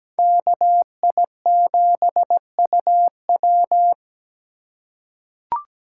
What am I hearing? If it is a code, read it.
Morse code, heard as a keyed tone, about 17 wpm, 700 Hz. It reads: KI7UW
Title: USA Call Signs - 5 Characters - 4x Character Spacing - 17wpm